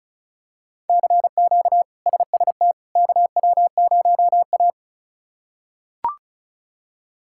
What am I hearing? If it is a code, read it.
Morse code, heard as a keyed tone, about 36 wpm, 700 Hz. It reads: CQ SST KW0A
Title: K1USN SST QSO Practice - 35wpm